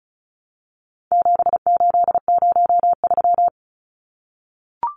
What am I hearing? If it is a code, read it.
Morse code, heard as a keyed tone, about 34 wpm, 700 Hz. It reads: 7803
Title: Numbers - 4-Digit 35wpm